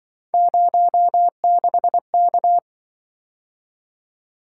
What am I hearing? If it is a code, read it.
Morse code, heard as a keyed tone, about 26 wpm, 700 Hz. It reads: 06K